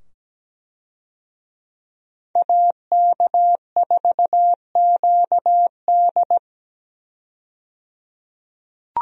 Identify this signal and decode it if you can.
Morse code, heard as a keyed tone, about 17 wpm, 700 Hz. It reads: AK4QD
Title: USA Call Signs - 5 Characters - 17wpm